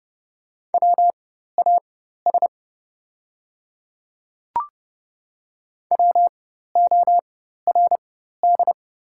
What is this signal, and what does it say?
Morse code, heard as a keyed tone, about 30 wpm, spaced as for 14 wpm, 690 Hz. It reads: WAS WORD
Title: 4x Character Spacing - Top 100 Words - 30wpm